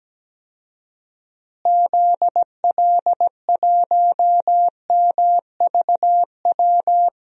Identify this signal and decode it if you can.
Morse code, heard as a keyed tone, about 17 wpm, 700 Hz. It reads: ZL1MVW
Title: DXCC-100 Call Signs - 6 Characters - Repeated 3x - 17wpm